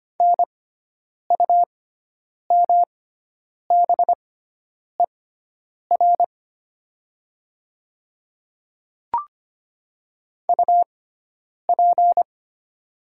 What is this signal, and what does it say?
Morse code, heard as a keyed tone, about 25 wpm, spaced as for 9 wpm, 700 Hz. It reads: NUMBER UP
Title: 6x Character Spacing - Top 100 Words - 25wpm